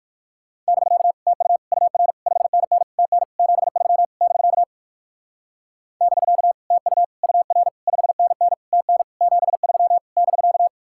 Morse code, heard as a keyed tone, about 56 wpm, 700 Hz. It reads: <BK> TU UR 5NN TN 73 <BK> <BK> TU UR 5NN TN 73 <BK>
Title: Speed-Racing Common QSO Phrases - 45wpm